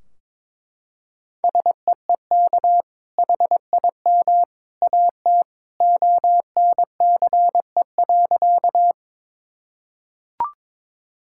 Morse code, heard as a keyed tone, about 22 wpm, 700 Hz. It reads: SEEK HIM AT ONCE.
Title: Sentences from Top 1600 Words 22wpm